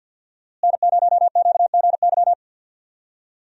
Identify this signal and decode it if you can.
Morse code, heard as a keyed tone, about 50 wpm, 700 Hz. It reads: N0QGY
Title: USA Call Signs - 5 Characters - Repeated 2x - 50wpm